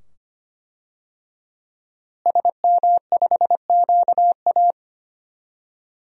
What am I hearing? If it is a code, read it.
Morse code, heard as a keyed tone, about 25 wpm, 700 Hz. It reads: SM5QA
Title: DXCC-100 Call Signs - 5 Characters - Repeated 2x - 25wpm